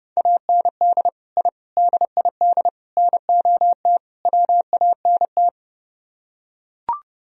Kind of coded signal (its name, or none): Morse code